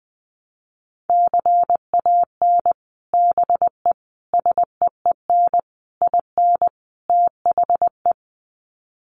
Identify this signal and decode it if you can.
Morse code, heard as a keyed tone, about 20 wpm, 700 Hz. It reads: CAN BE SEEN IN THE
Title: Sets of 5 Words Encore2 20wpm